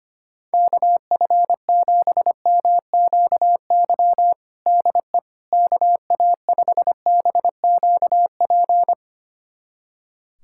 Morse code, heard as a keyed tone, about 25 wpm, 700 Hz. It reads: KF7MQY DE KA5BQP
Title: USA Callsign DE Callsign - 6 Characters - 25wpm